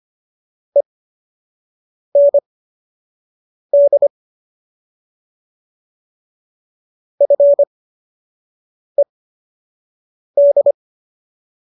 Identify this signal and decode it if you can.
Morse code, heard as a keyed tone, about 25 wpm, spaced as for 6 wpm, 570 Hz. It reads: END FED